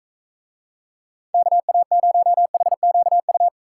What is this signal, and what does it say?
Morse code, heard as a keyed tone, about 45 wpm, 700 Hz. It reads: KA0HQU